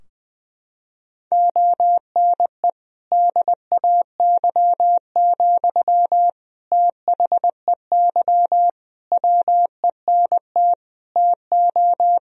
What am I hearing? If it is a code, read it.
Morse code, heard as a keyed tone, about 21 wpm, 700 Hz. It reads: ONE DAY, THEY WENT TO
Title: Sentences from Top 1000 Words - Review Part 2 - 20wpm